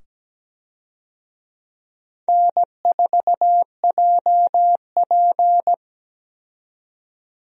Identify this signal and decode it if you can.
Morse code, heard as a keyed tone, about 17 wpm, 700 Hz. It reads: N4JP
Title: USA Call Signs - 4 Characters - 17wpm